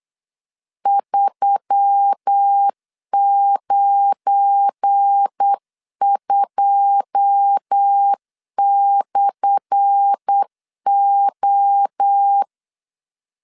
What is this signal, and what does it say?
Morse code, heard as a keyed tone, about 8 wpm, 800 Hz. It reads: 392/O
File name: LEZIONI / 10 SIGNS / 10A SLASH 8wpm 3452